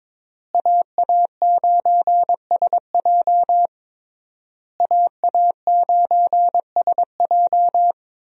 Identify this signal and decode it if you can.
Morse code, heard as a keyed tone, about 22 wpm, 700 Hz. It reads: AA9SJ AA9SJ
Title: USA Call Signs - 5 Characters - Repeated 3x - 22wpm